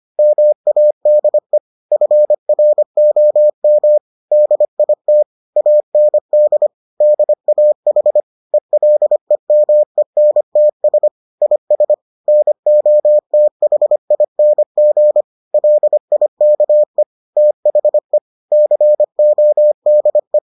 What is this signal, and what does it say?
Morse code, heard as a keyed tone, about 25 wpm, 600 Hz. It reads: MADE FROM DIT AND DAH ELEMENTS IS NOTHING LIKE THE CODE